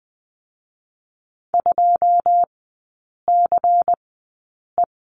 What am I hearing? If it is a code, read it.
Morse code, heard as a keyed tone, about 20 wpm, 700 Hz. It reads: 2 C E
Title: 107 Lesson 18e - Letter m - Three Characters 20wpm